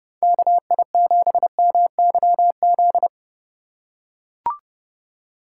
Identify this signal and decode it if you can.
Morse code, heard as a keyed tone, about 30 wpm, 700 Hz. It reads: KI7MYZ
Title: USA Call Signs - 6 Characters - 3x Character Spacing - 30wpm